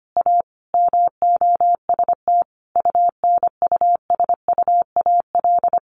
Morse code, heard as a keyed tone, about 25 wpm, 700 Hz. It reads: A MOST UNUSUAL
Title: Sentences from Top 1400 Words 25wpm